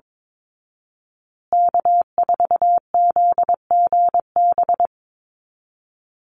Morse code, heard as a keyed tone, about 22 wpm, 700 Hz. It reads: K4ZGB
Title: Call Signs Repeated 2x - 22wpm